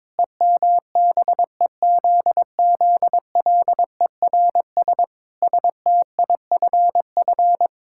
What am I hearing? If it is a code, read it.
Morse code, heard as a keyed tone, about 22 wpm, 700 Hz. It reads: EMBEZZLERS STIFF
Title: Pangram Challenge v2 - 22wpm